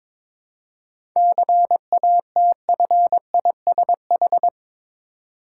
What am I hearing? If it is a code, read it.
Morse code, heard as a keyed tone, about 22 wpm, 690 Hz. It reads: CATFISH
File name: Difficult Words with Hints - 22wpm